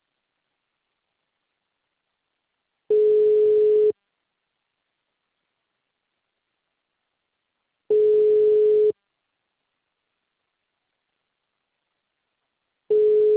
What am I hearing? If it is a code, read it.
Ringback tone (Europe, 425 Hz).